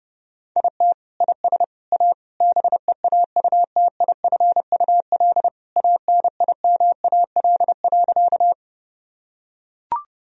Morse code, heard as a keyed tone, about 32 wpm, 700 Hz. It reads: IT IS A BEAUTIFUL ANIMAL.
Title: Sentences from Top 1000 Words - Review Part 1 - 30wpm